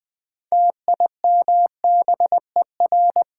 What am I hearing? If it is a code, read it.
Morse code, heard as a keyed tone, about 20 wpm, 700 Hz. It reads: TIMBER
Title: Build A Word - 2 Letters At A Time - 6-Letter Words - 20wpm